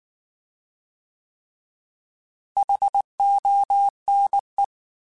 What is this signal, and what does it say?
Morse code, heard as a keyed tone, about 19 wpm, 800 Hz. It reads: HONE